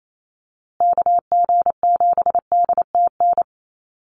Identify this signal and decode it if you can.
Morse code, heard as a keyed tone, about 28 wpm, 700 Hz. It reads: KG7DTN
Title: USA Call Signs Repeated 2x 28wpm